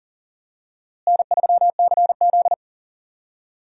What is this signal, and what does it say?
Morse code, heard as a keyed tone, about 40 wpm, 700 Hz. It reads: N3CZ
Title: USA Call Signs - 4 Characters - Repeated 2x - 40wpm